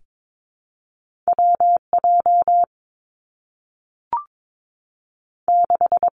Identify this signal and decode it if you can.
Morse code, heard as a keyed tone, about 22 wpm, 700 Hz. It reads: W J 6
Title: Two Challenging Letter-Number Combinations 22wpm